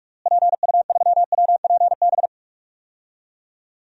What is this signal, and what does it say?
Morse code, heard as a keyed tone, about 41 wpm, 700 Hz. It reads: PU3WPB
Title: DXCC-100 Call Signs - 6 Characters - Repeated 3x - 45wpm